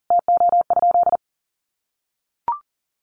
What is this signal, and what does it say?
Morse code, heard as a keyed tone, about 40 wpm, 700 Hz. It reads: TO?
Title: Sentences from Top 100 Words 40wpm